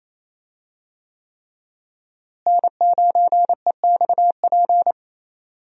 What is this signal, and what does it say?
Morse code, heard as a keyed tone, about 28 wpm, 700 Hz. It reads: N9EXP
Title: USA Call Signs - 5 Characters - Repeated 3x - 28wpm